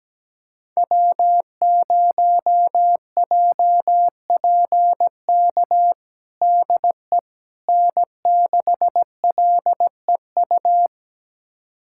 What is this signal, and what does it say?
Morse code, heard as a keyed tone, about 17 wpm, 700 Hz. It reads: W0JPK DE N6LEU
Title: USA Callsign DE Callsign - 5 Characters - 17wpm